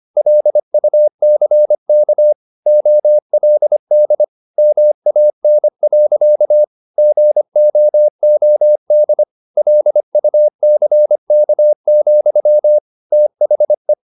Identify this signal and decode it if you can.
Morse code, heard as a keyed tone, about 25 wpm, 600 Hz. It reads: LUCK OLD MAN. GOOD LUCK, THE
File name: the old man and the sea part 1 25wpm